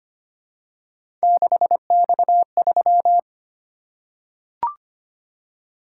Morse code, heard as a keyed tone, about 25 wpm, 700 Hz. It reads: 6X3
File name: Three Letter-Number Combinations - 25wpm